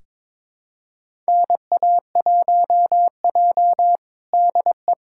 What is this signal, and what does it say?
Morse code, heard as a keyed tone, about 22 wpm, 700 Hz. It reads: NA1J DE
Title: USA Callsign DE Callsign - 4 Characters - 22wpm